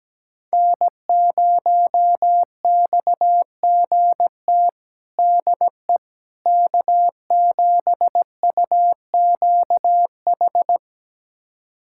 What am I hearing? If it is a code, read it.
Morse code, heard as a keyed tone, about 17 wpm, 700 Hz. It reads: N0XGT DE K7UQH
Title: USA Callsign DE Callsign - 5 Characters - 17wpm